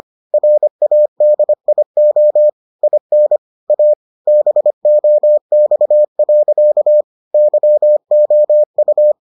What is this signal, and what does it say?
Morse code, heard as a keyed tone, about 25 wpm, 600 Hz. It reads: RADIO IN A BOX. YOU